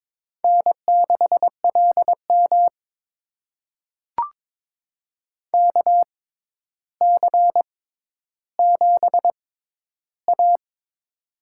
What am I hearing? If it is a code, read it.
Morse code, heard as a keyed tone, about 22 wpm, spaced as for 11 wpm, 700 Hz. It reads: N6LM KC7A
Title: USA Call Signs - 4 Characters - 6x Character Spacing - 22wpm